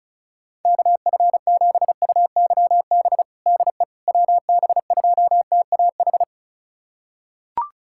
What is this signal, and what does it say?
Morse code, heard as a keyed tone, about 36 wpm, 690 Hz. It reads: KF7UYB DE WB2TAH
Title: USA Callsign DE Callsign - 6 Characters - 35wpm